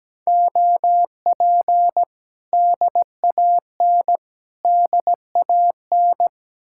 Morse code, heard as a keyed tone, about 18 wpm, 700 Hz. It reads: OP DAN DAN
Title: Speed-Racing Common QSO Phrases - 17wpm